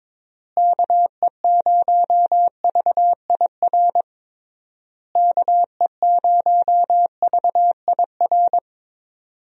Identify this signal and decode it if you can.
Morse code, heard as a keyed tone, about 22 wpm, 700 Hz. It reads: KE0VIR KE0VIR
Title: USA Call Signs - 6 Characters - Repeated 3x - 22wpm